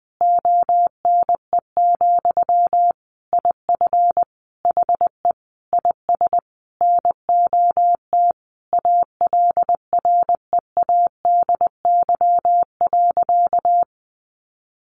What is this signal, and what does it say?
Morse code, heard as a keyed tone, about 20 wpm, 700 Hz. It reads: ONE, IF HE IS NOT ALREADY.
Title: Sentences from Top 900 Words 20wpm